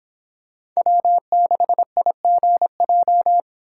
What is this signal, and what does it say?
Morse code, heard as a keyed tone, about 26 wpm, 700 Hz. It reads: W6IGJ